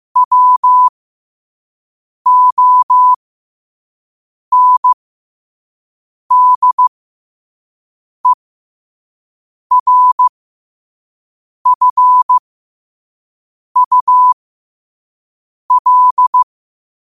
Morse code, heard as a keyed tone, about 15 wpm, spaced as for 5 wpm, 1000 Hz. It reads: WONDERFUL
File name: qotdfiles / 5wpm